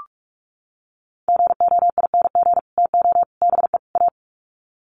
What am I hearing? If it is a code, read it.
Morse code, heard as a keyed tone, about 44 wpm, 700 Hz. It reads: GOING TO BE A
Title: Sets of 4 Words 45wpm